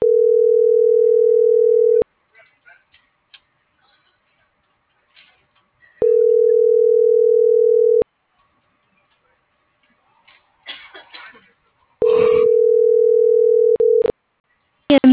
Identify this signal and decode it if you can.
Ringback tone (North America), 440 + 480 Hz.
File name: CENTRAL SOTOVA / out ricardo.lopez